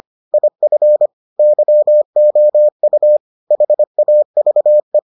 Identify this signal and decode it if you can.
Morse code, heard as a keyed tone, about 25 wpm, 600 Hz. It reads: IF YOU HAVE